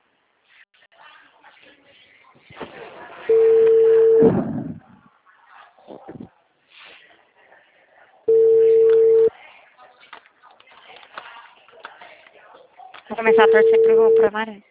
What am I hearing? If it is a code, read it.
Ringback tone (China, 450 Hz), with speech.